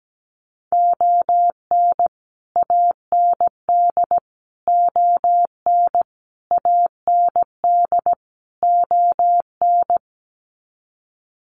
Morse code, heard as a keyed tone, about 17 wpm, 700 Hz. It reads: ON AND ON AND ON
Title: Sets of 5 Words Encore2 17wpm